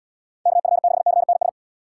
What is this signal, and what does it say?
Morse code, heard as a keyed tone, about 74 wpm, 690 Hz. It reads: KF6CNS